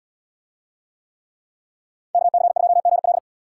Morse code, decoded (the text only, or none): KC2GL